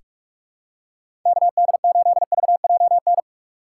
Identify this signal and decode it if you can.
Morse code, heard as a keyed tone, about 46 wpm, 700 Hz. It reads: KD8VJN
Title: USA Call Signs - 3x Character Spacing - 45wpm